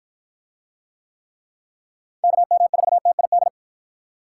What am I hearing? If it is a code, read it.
Morse code, heard as a keyed tone, about 48 wpm, 700 Hz. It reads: KM4TID